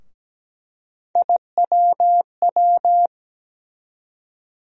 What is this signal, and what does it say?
Morse code, heard as a keyed tone, about 17 wpm, 700 Hz. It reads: IWW
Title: USA Call Sign Postfixes - 3 Letters 17wpm